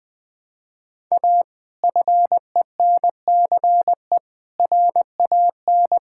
Morse code, heard as a keyed tone, about 20 wpm, 700 Hz. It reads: A FENCE RAN
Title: Sentences with 5-Word Phrases - 20wpm